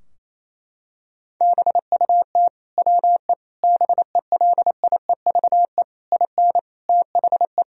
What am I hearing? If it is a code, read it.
Morse code, heard as a keyed tone, about 28 wpm, 700 Hz. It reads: BUT WE BELIEVE IN THE
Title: Sentences from Top 1000 Words - Review Part 3 - 28wpm